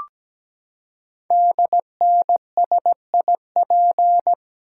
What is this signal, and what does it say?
Morse code, heard as a keyed tone, about 17 wpm, 700 Hz. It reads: DNSIP
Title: Code Groups - 5 Letters - 17wpm